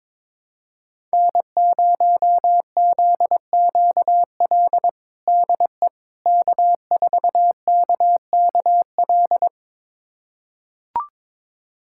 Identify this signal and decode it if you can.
Morse code, heard as a keyed tone, about 21 wpm, 700 Hz. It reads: N0ZQL DE K4KKL
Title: USA Callsign DE Callsign - 5 Characters - 22wpm